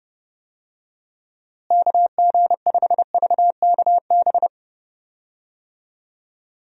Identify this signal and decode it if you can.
Morse code, heard as a keyed tone, about 30 wpm, 700 Hz. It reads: KG5VKB
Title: USA Call Signs - 6 Characters - Repeated 3x - 30wpm